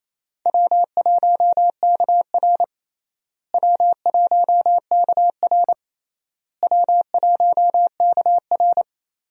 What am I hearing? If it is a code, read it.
Morse code, heard as a keyed tone, about 28 wpm, 700 Hz. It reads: W1KR W1KR W1KR
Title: USA Call Signs - 4 Characters - Repeated 3x - 28wpm